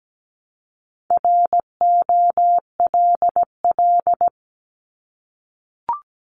Morse code, heard as a keyed tone, about 17 wpm, 700 Hz. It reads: ROLL
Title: Multi-letter Phonemes 17wpm